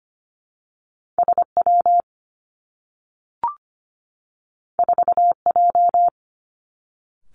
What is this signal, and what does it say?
Morse code, heard as a keyed tone, about 25 wpm, 700 Hz. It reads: S W 4 J
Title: Two Challenging Letter-Number Combinations 25wpm